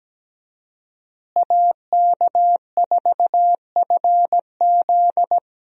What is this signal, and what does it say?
Morse code, heard as a keyed tone, about 17 wpm, 700 Hz. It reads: AK4FZ
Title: USA Call Signs - 5 Characters - Repeated 2x - 17wpm